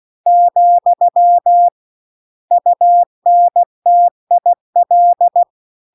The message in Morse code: , UNTIL

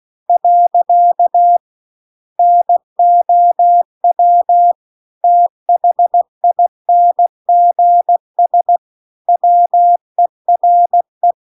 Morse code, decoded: . NOW THINGS WERE

